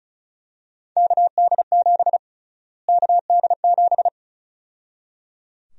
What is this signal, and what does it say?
Morse code, heard as a keyed tone, about 36 wpm, 700 Hz. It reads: KD7 KD7
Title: USA Call Sign Prefixes - Repeated 2x - 35wpm